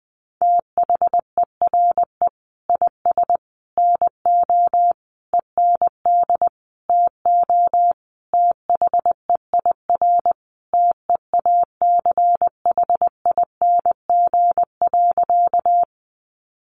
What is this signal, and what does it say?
Morse code, heard as a keyed tone, about 20 wpm, 700 Hz. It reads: THERE IS NO END TO THEIR TEACHING.
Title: Sentences from Top 1100 Words 20wpm